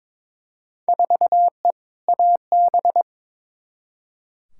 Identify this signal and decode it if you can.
Morse code, heard as a keyed tone, about 22 wpm, 700 Hz. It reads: 4E AB
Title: Field Day Exchanges Encore - 22wpm